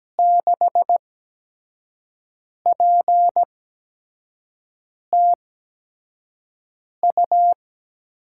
Morse code, heard as a keyed tone, about 17 wpm, spaced as for 5 wpm, 700 Hz. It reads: 6PTU
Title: USA Call Signs - 6 Characters - 8x Character Spacing - 17wpm